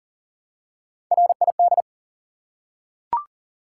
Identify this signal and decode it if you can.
Morse code, heard as a keyed tone, about 39 wpm, 700 Hz. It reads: RID
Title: Three Letter Words - 40wpm